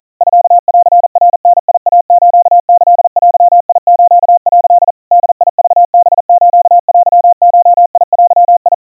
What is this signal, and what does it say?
Morse code (Japanese Wabun code), heard as a keyed tone, about 40 wpm, 700 Hz. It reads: ミンナダイスキデス」 ボクハステゴデ
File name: kawa / Morse sample / kamyu 150 cpm 700 Hz Crispy